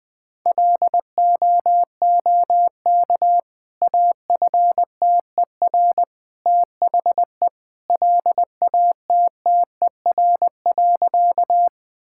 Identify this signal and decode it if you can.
Morse code, heard as a keyed tone, about 20 wpm, 700 Hz. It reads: LOOK AFTER THE LATTER.
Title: Sentences from Top 1600 Words 20wpm